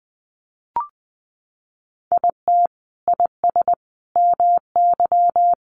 Morse code, heard as a keyed tone, about 20 wpm, 700 Hz. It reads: IT IS MY
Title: Sentences from Top 1000 Words 20wpm